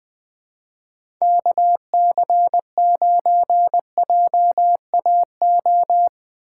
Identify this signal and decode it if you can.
Morse code, heard as a keyed tone, about 20 wpm, 700 Hz. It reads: KC9JAO
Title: USA Call Signs - 6 Characters - 6x Character Spacing - 20wpm